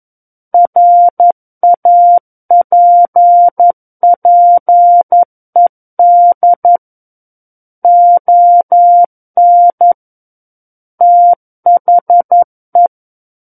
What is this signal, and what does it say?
Morse code, heard as a keyed tone, about 11 wpm, 700 Hz. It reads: RAPPED ON THE
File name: 11 WPM